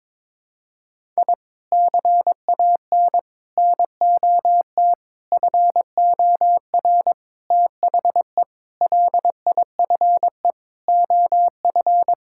Morse code, heard as a keyed tone, about 22 wpm, 700 Hz. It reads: I CAN NOT FOR THE LIFE OF
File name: practice / 22wpm / Sets of 8 Words - 22wpm